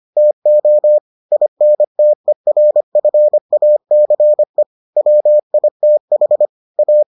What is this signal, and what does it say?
Morse code, heard as a keyed tone, about 25 wpm, 600 Hz. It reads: TO INTERFACE WITH A